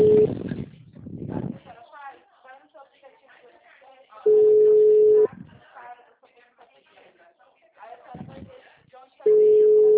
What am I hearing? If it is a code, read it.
Ringback tone (Europe, 425 Hz), with speech.
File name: out- -2003- - - .1801